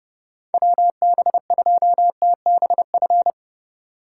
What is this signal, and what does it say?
Morse code, heard as a keyed tone, about 30 wpm, 700 Hz. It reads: WB2TBF
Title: USA Call Signs - 6 Characters - 3x Character Spacing - 30wpm